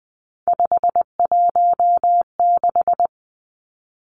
Morse code, heard as a keyed tone, about 20 wpm, 700 Hz. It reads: 516
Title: Three Challenging Letter-Number Combinations 20wpm